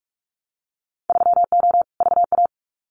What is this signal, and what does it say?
Morse code, heard as a keyed tone, about 44 wpm, 700 Hz. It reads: 3O VA
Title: Winter Field Day Exchanges 45wpm